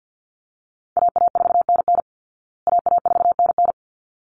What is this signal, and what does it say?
Morse code, heard as a keyed tone, about 48 wpm, 700 Hz. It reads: AA4NN AA4NN
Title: Call Signs Repeated 3x - 50wpm